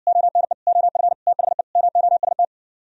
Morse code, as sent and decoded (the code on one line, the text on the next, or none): --- -. / --- ..-. / - .... / -- --- ... -
ONE OF THE MOST